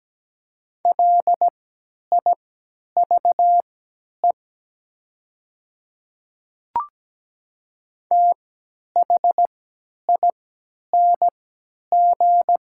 Morse code, heard as a keyed tone, about 17 wpm, spaced as for 10 wpm, 700 Hz. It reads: LIVE THING
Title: 3x Character Spacing - Top 200 Words - 17wpm